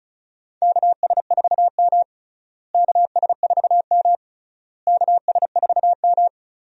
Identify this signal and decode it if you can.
Morse code, heard as a keyed tone, about 36 wpm, 700 Hz. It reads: KS4M KS4M KS4M
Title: USA Call Signs - 4 Characters - Repeated 3x - 35wpm